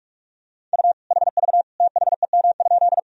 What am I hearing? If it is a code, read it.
Morse code, heard as a keyed tone, about 46 wpm, 700 Hz. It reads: U HV THEM?
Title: Sentences from Top 100 Words and Abbreviations 45wpm